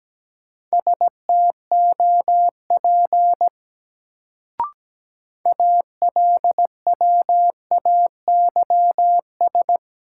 Morse code, heard as a keyed tone, about 17 wpm, 700 Hz. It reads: STOP ALWAYS
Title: Sentences from Top 300 Words - New Word Drill - 17wpm